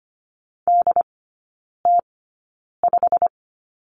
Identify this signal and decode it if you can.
Morse code, heard as a keyed tone, about 25 wpm, spaced as for 9 wpm, 700 Hz. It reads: DT5
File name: Lesson 17e - Letter u - Three Characters 25wpm